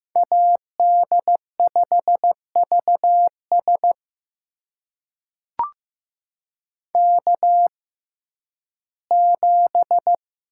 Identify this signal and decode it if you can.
Morse code, heard as a keyed tone, about 15 wpm, spaced as for 11 wpm, 700 Hz. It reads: AD5VS K7
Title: USA Call Signs - 5 Characters - 6x Character Spacing - 15wpm